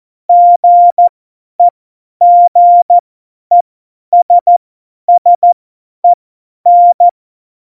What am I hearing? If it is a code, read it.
Morse code, heard as a keyed tone, about 14 wpm, spaced as for 10 wpm, 700 Hz. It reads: GEGESSEN
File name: Morsekurs / Witze / Witz 03.Dezember 2019, 50 F70